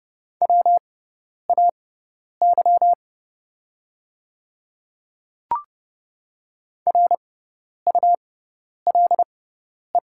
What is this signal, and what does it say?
Morse code, heard as a keyed tone, about 30 wpm, spaced as for 10 wpm, 690 Hz. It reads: WAY RULE
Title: 6x Character Spacing - Top 500 Words - 30wpm